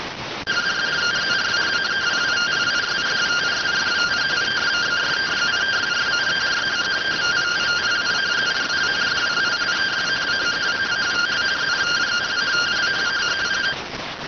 Two-tone teleprinter signal, about 170 Hz shift.